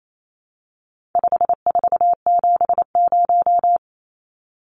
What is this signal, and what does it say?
Morse code, heard as a keyed tone, about 28 wpm, 700 Hz. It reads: 5470
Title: Numbers - 4-Digit 28wpm